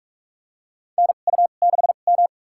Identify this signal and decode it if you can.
Morse code, heard as a keyed tone, about 42 wpm, spaced as for 31 wpm, 700 Hz. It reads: NU6M
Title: USA Call Signs - 4 Characters - 2x Character Spacing - 45wpm